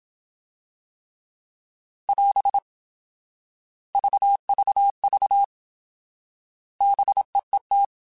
Morse code, heard as a keyed tone, about 27 wpm, 800 Hz. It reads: <AS> VVV BEET
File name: Alphabetical / Alphabetical mp3s / Bs @25